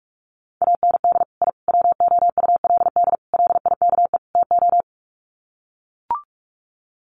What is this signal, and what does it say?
Morse code, heard as a keyed tone, about 44 wpm, 700 Hz. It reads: AND I WOULD LIKE TO
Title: Sets of 5 Words Encore 45wpm